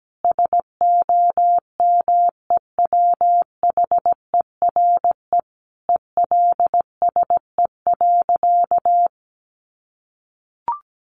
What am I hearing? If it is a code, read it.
Morse code, heard as a keyed tone, about 17 wpm, 700 Hz. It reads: SOMEWHERE ELSE.
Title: Sentences from Top 1400 Words 17wpm